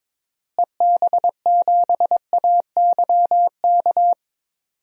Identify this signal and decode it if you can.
Morse code, heard as a keyed tone, about 22 wpm, 700 Hz. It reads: EB7AYK
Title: DXCC-100 Call Signs - 6 Characters - Repeated 3x - 22wpm